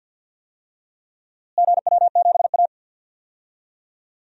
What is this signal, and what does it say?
Morse code, heard as a keyed tone, about 52 wpm, 700 Hz. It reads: GW7A
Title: DXCC-100 Call Signs - 4 Characters - Repeated 2x - 50wpm